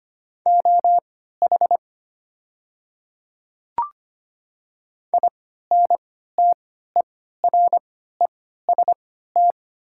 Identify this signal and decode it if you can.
Morse code, heard as a keyed tone, about 25 wpm, spaced as for 14 wpm, 700 Hz. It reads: OH INTEREST
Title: 3x Character Spacing - Top 500 Words - 25wpm